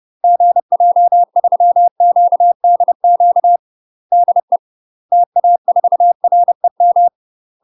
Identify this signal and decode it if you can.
Morse code, heard as a keyed tone, about 30 wpm, 700 Hz. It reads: GJ3QDQ DE TA4REM
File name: UK MASTER QSOs TRAINING / 30WPM / UK MASTER QSOs 0050